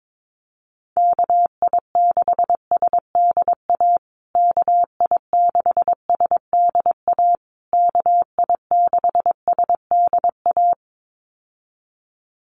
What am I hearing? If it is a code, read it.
Morse code, heard as a keyed tone, about 22 wpm, 700 Hz. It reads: KI6SDA KI6SDA KI6SDA
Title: USA Call Signs Repeated 3x 22wpm